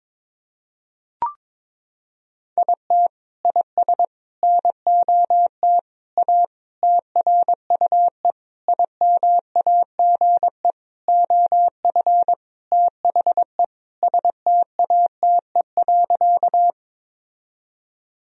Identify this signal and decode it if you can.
Morse code, heard as a keyed tone, about 23 wpm, 700 Hz. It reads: IT IS NOT A TRUE IMAGE OF THE STATE.
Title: Sentences from Top 1000 Words - Review Part 4 - 22wpm